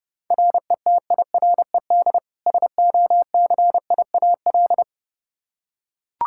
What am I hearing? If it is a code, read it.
Morse code, heard as a keyed tone, about 30 wpm, 700 Hz. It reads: RETIRED SOCIAL
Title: Practice QSO - Exchange 3 - More Info - 30wpm